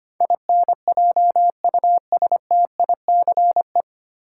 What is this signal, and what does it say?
Morse code, heard as a keyed tone, about 25 wpm, 700 Hz. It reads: INJUSTICE
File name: Nine Letter Words - 25wpm